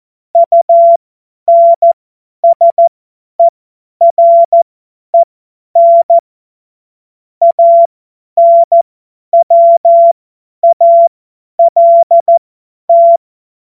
Morse code, heard as a keyed tone, about 14 wpm, spaced as for 10 wpm, 680 Hz. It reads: UNSEREN ANWALT
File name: Morsekurs / Witze / Witz 15. August 2023 50 F70